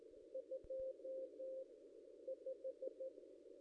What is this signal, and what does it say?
Morse code, heard as a keyed tone, about 14 wpm, spaced as for 9 wpm, 530 Hz. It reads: 25
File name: contests / ss-cw-k5tr-op-n5rz -1600